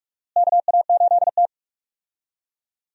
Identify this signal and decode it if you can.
Morse code, heard as a keyed tone, about 46 wpm, 700 Hz. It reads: KA8T
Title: USA Call Signs - 4 Characters - Repeated 3x - 45wpm